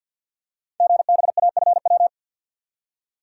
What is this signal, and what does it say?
Morse code, heard as a keyed tone, about 48 wpm, 700 Hz. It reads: GBAFW
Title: Code Groups - 5 Letters - 50wpm